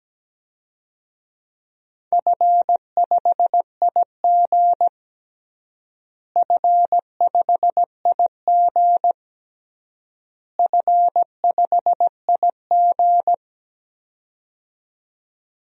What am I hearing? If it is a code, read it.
Morse code, heard as a keyed tone, about 17 wpm, 700 Hz. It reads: F5IG F5IG F5IG
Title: DXCC-100 Call Signs - 4 Characters - Repeated 3x - 17wpm